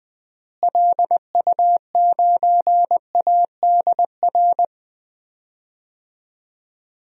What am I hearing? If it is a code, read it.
Morse code, heard as a keyed tone, about 20 wpm, 700 Hz. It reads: LU9ADR
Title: DXCC-100 Call Signs - 6 Characters - Repeated 2x - 20wpm